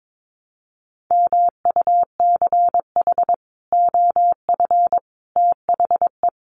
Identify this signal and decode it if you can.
Morse code, heard as a keyed tone, about 22 wpm, 700 Hz. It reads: MUCH OF THE
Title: Sets of 3 Words 22wpm